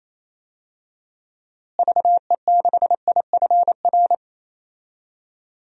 Morse code, heard as a keyed tone, about 28 wpm, 700 Hz. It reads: VE6IFR
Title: DXCC-100 Call Signs - 6 Characters - Repeated 3x - 28wpm